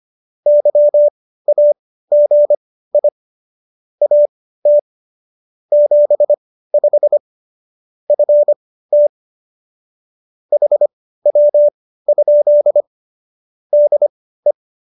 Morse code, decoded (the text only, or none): YAGI AT 75 FT HW? DE